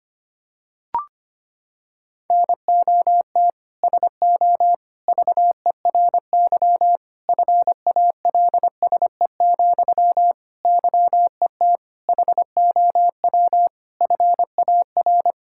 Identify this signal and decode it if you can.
Morse code, heard as a keyed tone, about 25 wpm, 700 Hz. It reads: NOT SO VERY FALSE, YET HOW FAR
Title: Sentences from Top 1600 Words 25wpm